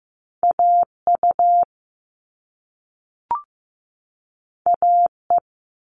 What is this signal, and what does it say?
Morse code, heard as a keyed tone, about 15 wpm, 700 Hz. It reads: AU AE
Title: All Cut Numbers - 2-Digit 15wpm